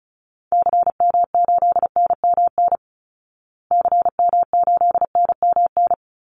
Morse code, heard as a keyed tone, about 34 wpm, 700 Hz. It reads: CM8NMN CM8NMN
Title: Call Signs Repeated 2x - 35wpm